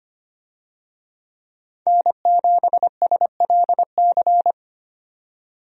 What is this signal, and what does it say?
Morse code, heard as a keyed tone, about 25 wpm, 690 Hz. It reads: N7SLC
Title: USA Call Signs - 5 Characters - 25wpm